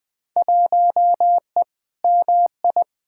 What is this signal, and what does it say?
Morse code, heard as a keyed tone, about 20 wpm, 700 Hz. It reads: 1E MI
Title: Field Day Exchanges - 20wpm